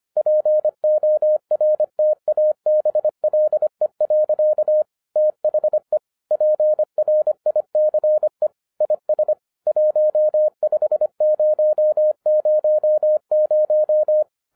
Morse code, heard as a keyed tone, about 25 wpm, 610 Hz. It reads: PORTABLE. THE PRICE IS 15000